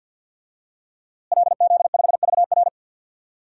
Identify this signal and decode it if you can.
Morse code, heard as a keyed tone, about 51 wpm, 700 Hz. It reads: LZ5VR
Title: DXCC-100 Call Signs - 5 Characters - Repeated 2x - 50wpm